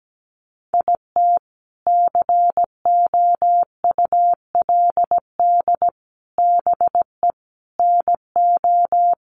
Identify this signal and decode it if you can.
Morse code, heard as a keyed tone, about 17 wpm, 700 Hz. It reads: IT COULD BE NO
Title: Sentences from Top 800 Words 17wpm